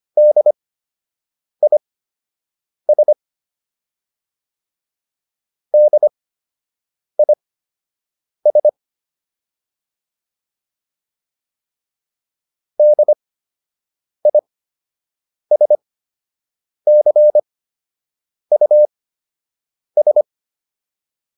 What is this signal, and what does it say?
Morse code, heard as a keyed tone, about 25 wpm, spaced as for 7 wpm, 600 Hz. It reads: DIS DIS DISCUS